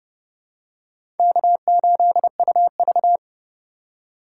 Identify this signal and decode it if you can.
Morse code, heard as a keyed tone, about 30 wpm, 700 Hz. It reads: K8UV
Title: USA Call Signs - 2x Character Spacing - 30wpm